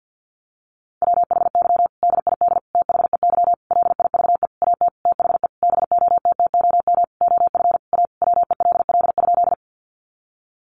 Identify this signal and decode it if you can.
Morse code, heard as a keyed tone, about 50 wpm, 700 Hz. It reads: WHY DID THEY LIVE AT THE BOTTOM OF A WELL?
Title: Sentences from Top 900 Words 50wpm